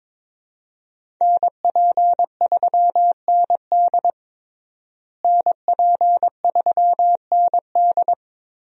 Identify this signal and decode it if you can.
Morse code, heard as a keyed tone, about 22 wpm, 700 Hz. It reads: NP3ND NP3ND
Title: USA Call Signs - 5 Characters - Repeated 3x - 22wpm